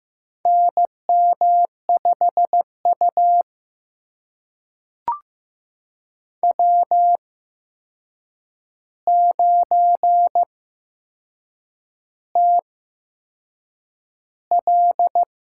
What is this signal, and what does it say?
Morse code, heard as a keyed tone, about 15 wpm, spaced as for 6 wpm, 700 Hz. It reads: NM5U W9TL
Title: USA Call Signs - 4 Characters - 8x Character Spacing - 15wpm